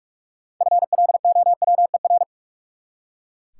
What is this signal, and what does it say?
Morse code, heard as a keyed tone, about 43 wpm, 700 Hz. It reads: FLOWER
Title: Six Letter Words - 45wpm